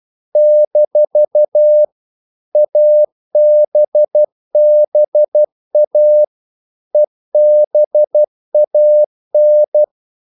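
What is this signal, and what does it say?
Morse code, heard as a keyed tone, about 12 wpm, 600 Hz. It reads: - ABBA EBAN
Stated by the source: Morse practice sending English quotations.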